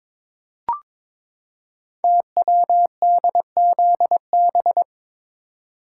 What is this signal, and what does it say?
Morse code, heard as a keyed tone, about 22 wpm, 700 Hz. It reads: TWDZB
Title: Code Groups - 5 Letters - 22wpm